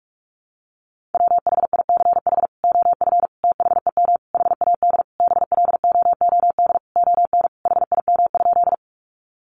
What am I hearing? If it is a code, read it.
Morse code, heard as a keyed tone, about 44 wpm, 700 Hz. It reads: WHICH OF THEM HAD BLOOD ON HIM?
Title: Sentences from Top 1100 Words 45wpm